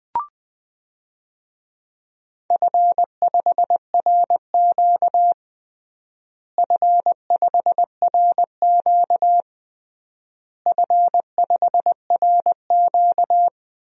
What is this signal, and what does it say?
Morse code, heard as a keyed tone, about 20 wpm, 700 Hz. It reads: F5RQ F5RQ F5RQ
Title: DXCC-100 Call Signs - 4 Characters - Repeated 3x - 20wpm